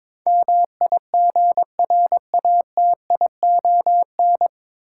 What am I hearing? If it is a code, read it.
Morse code, heard as a keyed tone, about 22 wpm, 700 Hz. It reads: MIGRATION
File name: 9 Letter Words - First 3 Letters Given - 22wpm